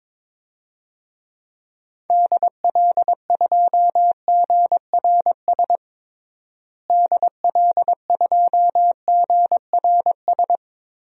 Morse code, heard as a keyed tone, about 22 wpm, 700 Hz. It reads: DL2GRS DL2GRS
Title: DXCC-100 Call Signs - 6 Characters - Repeated 3x - 22wpm